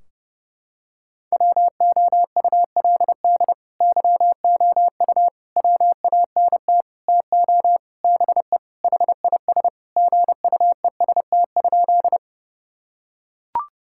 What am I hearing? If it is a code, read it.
Morse code, heard as a keyed tone, about 30 wpm, 700 Hz. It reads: WOULD YOU WANT TO BE HIS GUEST?
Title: Sentences from Top 1600 Words 30wpm